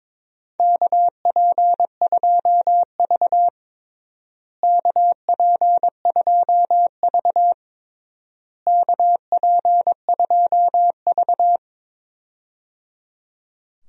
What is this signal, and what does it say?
Morse code, heard as a keyed tone, about 22 wpm, 700 Hz. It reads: KP2V KP2V KP2V
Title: DXCC-100 Call Signs - 4 Characters - Repeated 3x - 22wpm